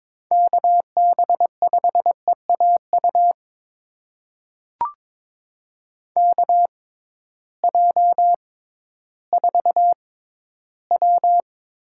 Morse code, heard as a keyed tone, about 22 wpm, spaced as for 13 wpm, 700 Hz. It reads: KB5EAU KJ4W
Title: USA Call Signs - 6 Characters - 6x Character Spacing - 22wpm